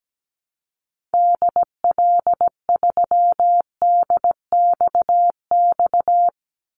Morse code, heard as a keyed tone, about 17 wpm, 700 Hz. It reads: DL3DXX
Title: Call Signs Repeated 3x - 17wpm